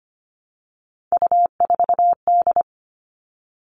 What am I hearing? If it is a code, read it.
Morse code, heard as a keyed tone, about 25 wpm, 700 Hz. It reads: U4D
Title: All Cut Numbers - 3-Digit 25wpm